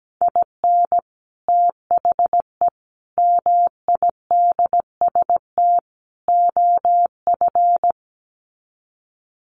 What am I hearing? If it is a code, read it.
Morse code, heard as a keyed tone, about 17 wpm, 700 Hz. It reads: IN THE MIDST OF
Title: Sets of 4 Words 17wpm